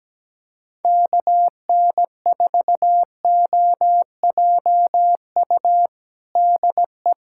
Morse code, heard as a keyed tone, about 17 wpm, 700 Hz. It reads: KN4OJU DE
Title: USA Callsign DE Callsign 17wpm